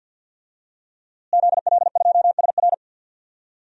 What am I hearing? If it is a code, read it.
Morse code, heard as a keyed tone, about 47 wpm, 700 Hz. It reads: ZL2SR
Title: DXCC-100 Call Signs - 5 Characters - Repeated 3x - 50wpm